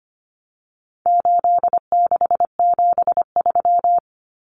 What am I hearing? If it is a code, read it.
Morse code, heard as a keyed tone, about 25 wpm, 700 Hz. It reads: 8673
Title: Common Cut Numbers - 4-Digit 25wpm